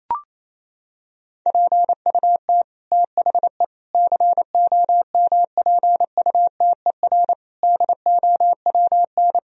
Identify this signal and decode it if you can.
Morse code, heard as a keyed tone, about 28 wpm, 700 Hz. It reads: PUT THE COMPUTER DOWN
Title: Sentences from Top 1000 Words - Review Part 3 - 28wpm